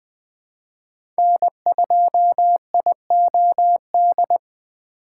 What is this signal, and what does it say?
Morse code, heard as a keyed tone, about 20 wpm, 700 Hz. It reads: N2IOD
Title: USA Call Signs - 5 Characters - 4x Character Spacing - 20wpm